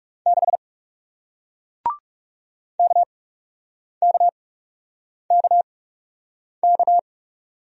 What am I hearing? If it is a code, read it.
Morse code, heard as a keyed tone, about 38 wpm, spaced as for 8 wpm, 700 Hz. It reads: 6 KKKK